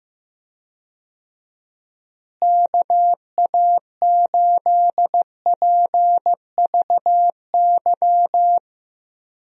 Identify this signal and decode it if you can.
Morse code, heard as a keyed tone, about 15 wpm, 700 Hz. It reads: KA8PVY
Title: USA Call Signs - 6 Characters - Repeated 3x - 15wpm